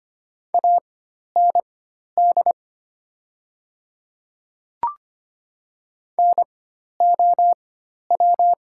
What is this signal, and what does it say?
Morse code, heard as a keyed tone, about 25 wpm, spaced as for 12 wpm, 700 Hz. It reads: AND NOW
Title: 4x Character Spacing - Top 100 Words - 25wpm